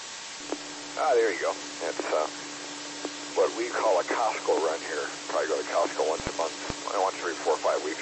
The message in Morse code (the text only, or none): none